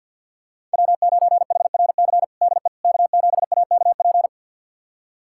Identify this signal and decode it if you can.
Morse code, heard as a keyed tone, about 52 wpm, 700 Hz. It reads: W9HRC DE K7AKP
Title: USA Callsign DE Callsign - 5 Characters - 50wpm